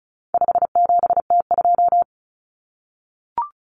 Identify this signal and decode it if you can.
Morse code, heard as a keyed tone, about 34 wpm, 700 Hz. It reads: 57T2
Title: Common Cut Numbers - 4-Digit 35wpm